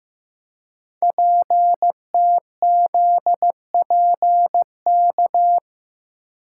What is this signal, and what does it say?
Morse code, heard as a keyed tone, about 15 wpm, 700 Hz. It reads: PTZPK
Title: Code Groups - 5 Letters - 15wpm